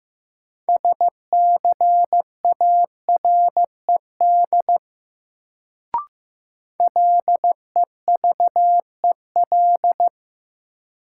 Morse code, heard as a keyed tone, about 15 wpm, 700 Hz. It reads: SCARED LEVEL
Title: Sentences from Top 700 Words - New Word Drill - 15wpm